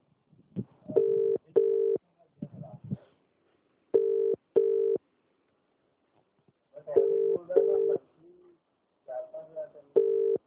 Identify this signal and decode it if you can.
Ringback tone (UK), 400 + 450 Hz.